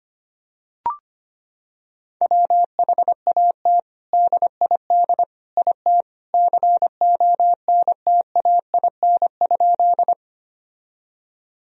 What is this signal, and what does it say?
Morse code, heard as a keyed tone, about 25 wpm, 700 Hz. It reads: WHAT DID IT CONTAIN?
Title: Sentences from Top 1000 Words - Review Part 2 - 25wpm